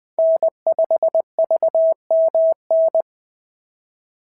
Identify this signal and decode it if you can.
Morse code, heard as a keyed tone, about 20 wpm, 650 Hz. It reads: N5VMN
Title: POTA QSO Practice - 20wpm